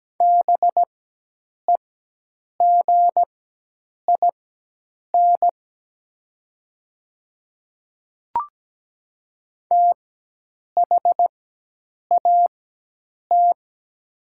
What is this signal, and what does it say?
Morse code, heard as a keyed tone, about 17 wpm, spaced as for 8 wpm, 690 Hz. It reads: BEGIN THAT
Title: 4x Character Spacing - Top 300 Words - 17wpm